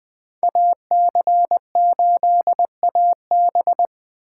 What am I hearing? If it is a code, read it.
Morse code, heard as a keyed tone, about 24 wpm, 700 Hz. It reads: AC8AB
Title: Speed Racing - USA Call Signs 20wpm